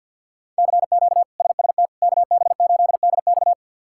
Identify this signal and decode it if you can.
Morse code, heard as a keyed tone, about 53 wpm, 700 Hz. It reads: CQ SST KB8DX
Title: K1USN SST QSO Practice - 50wpm